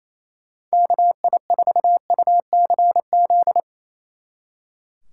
Morse code, heard as a keyed tone, about 37 wpm, 700 Hz. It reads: KI4UCZ